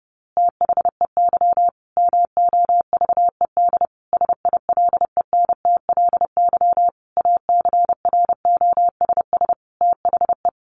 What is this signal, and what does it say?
Morse code, heard as a keyed tone, about 30 wpm, 700 Hz. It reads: THEY MOVED SILENTLY ACROSS THE
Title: A Sound Of Thunder 30wpm